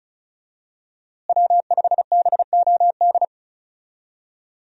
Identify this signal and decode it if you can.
Morse code, heard as a keyed tone, about 36 wpm, 700 Hz. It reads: W5BOD
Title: USA Call Signs - 5 Characters - Repeated 3x - 35wpm